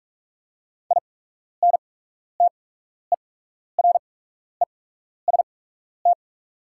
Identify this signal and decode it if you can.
Morse code, heard as a keyed tone, about 44 wpm, spaced as for 12 wpm, 690 Hz. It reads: INTEREST
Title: 8x Character Spacing - Top 500 Words - 45wpm